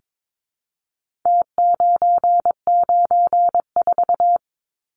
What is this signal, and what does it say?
Morse code, heard as a keyed tone, about 22 wpm, 700 Hz. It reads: T994
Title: Numbers - Contest Style - 4-Digit 22wpm